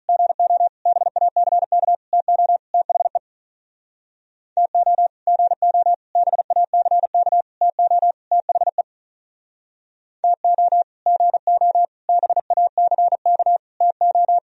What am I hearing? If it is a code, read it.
Morse code, heard as a keyed tone, about 44 wpm, 700 Hz. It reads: GO BACK TO THE TO GO BACK TO THE TO GO BACK TO